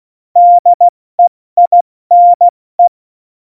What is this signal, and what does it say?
Morse code, heard as a keyed tone, about 16 wpm, 700 Hz. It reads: DEINE
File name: Morsekurs / Witze / Witz 28. August 2018, 70 F80